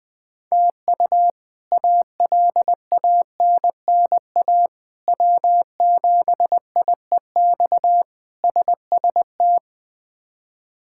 Morse code, heard as a keyed tone, about 20 wpm, 700 Hz. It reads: TU ALANNA W7IEX SST
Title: K1USN SST QSO Practice - 20wpm